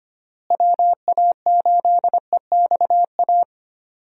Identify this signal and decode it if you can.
Morse code, heard as a keyed tone, about 25 wpm, 700 Hz. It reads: WA8EXA
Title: USA Call Signs - 6 Characters - Repeated 2x - 25wpm